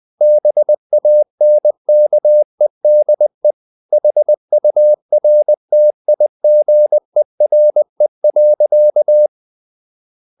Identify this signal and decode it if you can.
Morse code, heard as a keyed tone, about 20 wpm, 600 Hz. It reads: BANKEDE HURTIGERE.
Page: Ferie 20 wpm